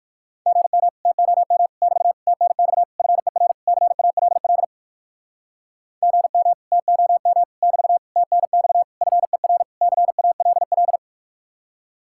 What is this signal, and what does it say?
Morse code, heard as a keyed tone, about 59 wpm, 700 Hz. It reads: GM TOM <BT> TNX FER CALL GM TOM <BT> TNX FER CALL